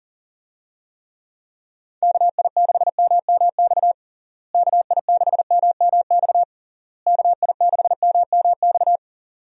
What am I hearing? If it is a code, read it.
Morse code, heard as a keyed tone, about 40 wpm, 700 Hz. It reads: KI6MMX KI6MMX KI6MMX
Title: USA Call Signs - 6 Characters - Repeated 3x - 40wpm